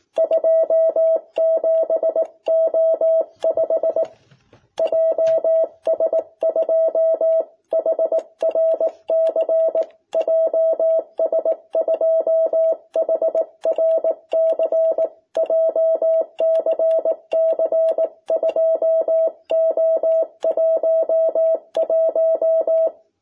Morse code, heard as a keyed tone, about 18 wpm, 650 Hz. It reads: 27O5 JS2HRC JS2HRC JCC 2O11
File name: jh8cbh / 2021cw